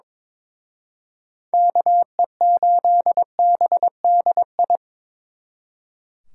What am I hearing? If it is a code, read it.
Morse code, heard as a keyed tone, about 26 wpm, 700 Hz. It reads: KE8BDI
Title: Speed Racing - USA Call Signs 22wpm